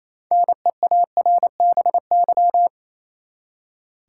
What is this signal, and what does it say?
Morse code, heard as a keyed tone, about 28 wpm, 700 Hz. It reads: NEARBY
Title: Six Letter Words - 28wpm